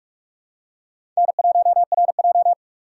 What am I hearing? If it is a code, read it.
Morse code, heard as a keyed tone, about 48 wpm, 700 Hz. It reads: N1RJ
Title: USA Call Signs - 4 Characters - 2x Character Spacing - 45wpm